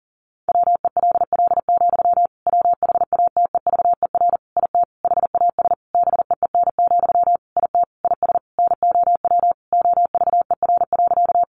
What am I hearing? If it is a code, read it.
Morse code, heard as a keyed tone, about 40 wpm, 700 Hz. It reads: WELL, WHATEVER IT HAS BEEN, IT IS NOW OVER.
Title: Sentences from Top 1100 Words 40wpm